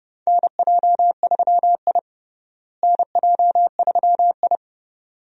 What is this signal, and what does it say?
Morse code, heard as a keyed tone, about 31 wpm, 700 Hz. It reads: NJ3I NJ3I
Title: USA Call Signs - 4 Characters - Repeated 2x - 30wpm